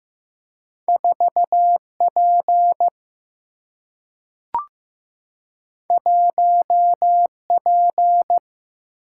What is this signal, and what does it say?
Morse code, heard as a keyed tone, about 15 wpm, 700 Hz. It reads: 4P 1P
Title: Single-Letter Single-Number 15wpm